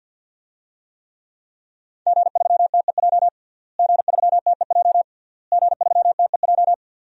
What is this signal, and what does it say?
Morse code, heard as a keyed tone, about 46 wpm, 700 Hz. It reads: G3TEJ G3TEJ G3TEJ
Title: DXCC-100 Call Signs - 5 Characters - Repeated 3x - 50wpm